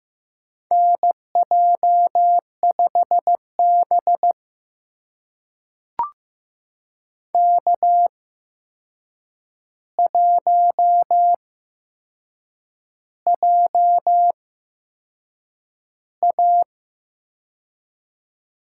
Morse code, heard as a keyed tone, about 15 wpm, spaced as for 6 wpm, 700 Hz. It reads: NJ5B K1JA
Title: USA Call Signs - 4 Characters - 8x Character Spacing - 15wpm